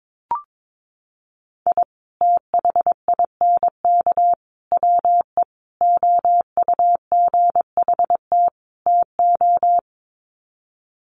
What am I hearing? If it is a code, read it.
Morse code, heard as a keyed tone, about 22 wpm, 700 Hz. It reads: I THINK WE OUGHT TO
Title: Sets of 5 Words Encore 22wpm